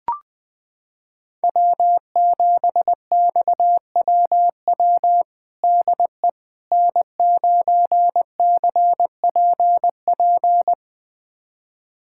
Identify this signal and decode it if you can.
Morse code, heard as a keyed tone, about 20 wpm, 700 Hz. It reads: W7XWW DE N9CPP
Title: USA Callsign DE Callsign - 5 Characters - 20wpm